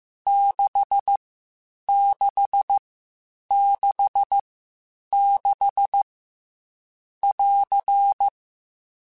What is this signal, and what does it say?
Morse code, heard as a keyed tone, about 15 wpm, spaced as for 8 wpm, 800 Hz. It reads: 6666 <AR>